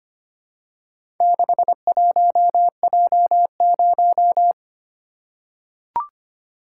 Morse code, heard as a keyed tone, about 25 wpm, 700 Hz. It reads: 61J0
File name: practice / 25wpm / Four Letter-Number Combinations - 25wpm